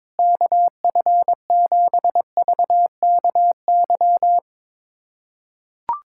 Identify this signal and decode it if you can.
Morse code, heard as a keyed tone, about 22 wpm, 700 Hz. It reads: KF7VKY
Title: USA Callsign DE Callsign 22wpm